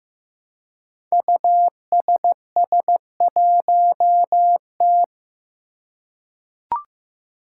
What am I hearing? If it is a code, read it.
Morse code, heard as a keyed tone, about 15 wpm, 700 Hz. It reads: USS1T
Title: Five Challenging Letter-Number Combinations 15wpm